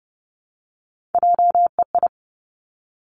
Morse code, heard as a keyed tone, about 30 wpm, 700 Hz. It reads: JEI
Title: Three Challenging Letter-Number Combinations 30wpm